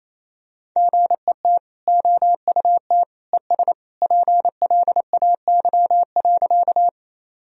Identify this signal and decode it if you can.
Morse code, heard as a keyed tone, about 28 wpm, 700 Hz. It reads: GET OUT ES PLAY.
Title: Sentences from Top 200 Words and Abbreviations 28wpm